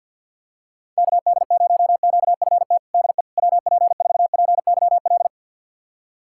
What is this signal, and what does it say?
Morse code, heard as a keyed tone, about 47 wpm, 700 Hz. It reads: KD0QFT DE WP4PYL
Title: USA Callsign DE Callsign - 6 Characters - 50wpm